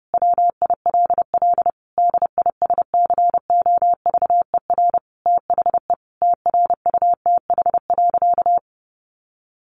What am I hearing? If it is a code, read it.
Morse code, heard as a keyed tone, about 30 wpm, 700 Hz. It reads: WILL DISCOVER THE TRUTH.
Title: Sentences from Top 1100 Words 30wpm